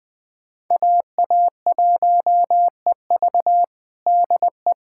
Morse code, heard as a keyed tone, about 20 wpm, 700 Hz. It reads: AA1EV DE
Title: USA Callsign DE Callsign - 5 Characters - 20wpm